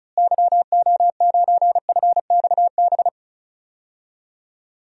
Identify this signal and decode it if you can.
Morse code, heard as a keyed tone, about 36 wpm, 700 Hz. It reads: YO9FXB
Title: DXCC-100 Call Signs - 6 Characters - Repeated 3x - 35wpm